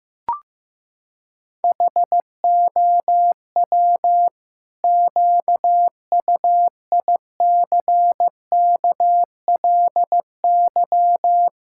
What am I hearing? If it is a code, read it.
Morse code, heard as a keyed tone, about 15 wpm, 700 Hz. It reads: HOW QUICKLY
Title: Pangram Challenge v2 - 15wpm